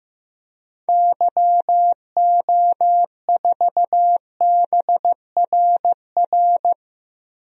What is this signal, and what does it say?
Morse code, heard as a keyed tone, about 15 wpm, 700 Hz. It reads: YO4BRR
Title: DXCC-100 Call Signs - 6 Characters - Repeated 2x - 15wpm